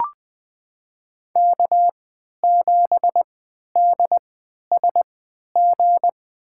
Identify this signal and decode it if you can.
Morse code, heard as a keyed tone, about 20 wpm, spaced as for 11 wpm, 700 Hz. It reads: K7DSG
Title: USA Call Signs - 5 Characters - 3x Character Spacing - 20wpm